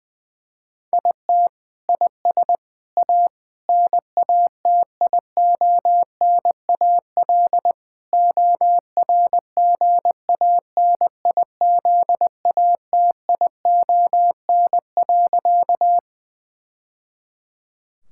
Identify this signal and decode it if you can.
Morse code, heard as a keyed tone, about 20 wpm, 700 Hz. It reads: IT IS A NATIONAL ORGANIZATION.
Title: Sentences from Top 1000 Words - Review Part 5 - 20wpm